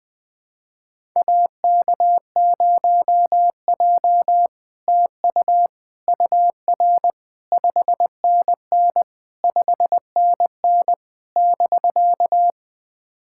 Morse code, heard as a keyed tone, about 21 wpm, 700 Hz. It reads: AK0J TU UR 5NN 5NN <BK>
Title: SOTA QSO Practice 20wpm